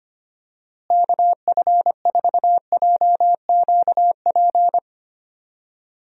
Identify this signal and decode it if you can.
Morse code, heard as a keyed tone, about 25 wpm, 700 Hz. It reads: KF4JQP
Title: USA Call Signs - 6 Characters - Repeated 2x - 25wpm